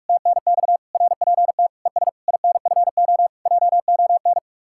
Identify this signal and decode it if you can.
Morse code, heard as a keyed tone, about 58 wpm, 700 Hz. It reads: TNX RPT ES INFO JON